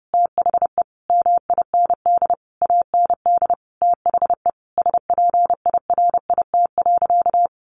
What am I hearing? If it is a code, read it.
Morse code, heard as a keyed tone, about 30 wpm, 700 Hz. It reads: THE MIND AND THE SPIRIT.
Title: Sentences from Top 1400 Words 30wpm